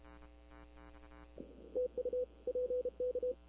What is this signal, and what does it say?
Morse code, heard as a keyed tone, about 31 wpm, 500 Hz. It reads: TU PK
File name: CW / 2005-k5tr arrl cw dx -20:00